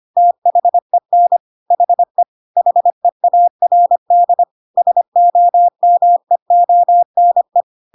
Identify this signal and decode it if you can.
Morse code, heard as a keyed tone, about 25 wpm, 700 Hz. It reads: THEN HE HEARD SOMEONE
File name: Pinocchio / Pinocchio ch4 25 25 ik1hge